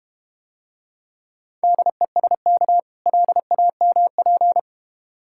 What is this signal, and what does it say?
Morse code, heard as a keyed tone, about 31 wpm, 700 Hz. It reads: DESK LAMP
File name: practice / 25wpm / Difficult Words - Speed-Racing with Hints - 25wpm